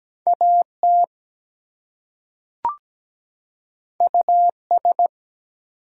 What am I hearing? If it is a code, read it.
Morse code, heard as a keyed tone, about 17 wpm, 700 Hz. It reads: AT US
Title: Two Letter Words - 17wpm